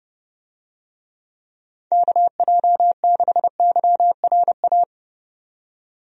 Morse code, heard as a keyed tone, about 30 wpm, 700 Hz. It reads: KJ6YRA
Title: USA Call Signs - 6 Characters - 30wpm